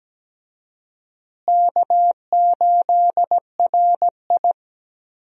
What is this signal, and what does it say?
Morse code, heard as a keyed tone, about 17 wpm, 700 Hz. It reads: K8RI
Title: USA Call Signs - 4 Characters - Repeated 3x - 17wpm